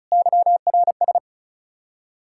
Morse code, heard as a keyed tone, about 44 wpm, 700 Hz. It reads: YRS